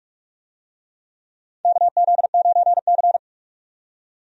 Morse code, heard as a keyed tone, about 46 wpm, 700 Hz. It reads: KZ9C
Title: USA Call Signs - 4 Characters - 45wpm